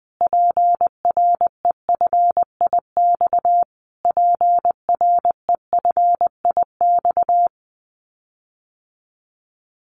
Morse code, heard as a keyed tone, about 20 wpm, 700 Hz. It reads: PREFIX PREFIX
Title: Callsign Prefixes for Top 25 DXCC - Repeated 3x - 20wpm